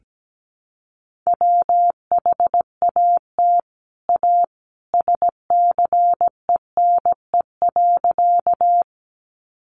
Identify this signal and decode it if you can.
Morse code, heard as a keyed tone, about 17 wpm, 700 Hz. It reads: WHAT A SCENE.
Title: Sentences from Top 1100 Words 17wpm